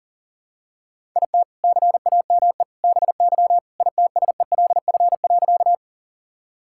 Morse code, heard as a keyed tone, about 37 wpm, 700 Hz. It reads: IT CAME BY ITSELF.
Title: Sentences from Top 1000 Words - Review Part 5 - 40wpm